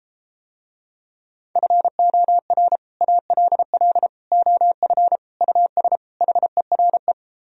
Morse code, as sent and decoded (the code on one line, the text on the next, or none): ..-. --- .-. / .- .-.. .-.. / --- ..-. / ..- ... / .... .-. .
FOR ALL OF US HERE